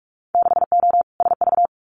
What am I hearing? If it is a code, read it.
Morse code, heard as a keyed tone, about 46 wpm, 700 Hz. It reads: 6O SV
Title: Winter Field Day Exchanges Encore 45wpm